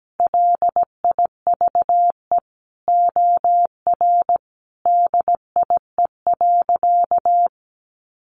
Morse code, heard as a keyed tone, about 17 wpm, 700 Hz. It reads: LIVE OR DIE.
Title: Sentences from Top 1300 Words 17wpm